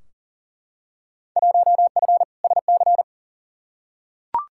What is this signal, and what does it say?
Morse code, heard as a keyed tone, about 40 wpm, 700 Hz. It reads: 1F SC
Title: Field Day Exchanges Encore - 40wpm